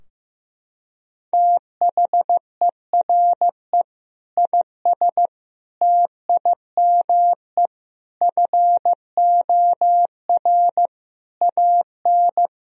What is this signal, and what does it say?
Morse code, heard as a keyed tone, about 15 wpm, 700 Hz. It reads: THERE IS TIME FOR AN
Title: Sentences from Top 1000 Words - Review Part 3 - 15wpm